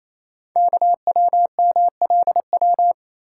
Morse code, heard as a keyed tone, about 27 wpm, 700 Hz. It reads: KWMLW
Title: Code Groups - 5 Letters - 28wpm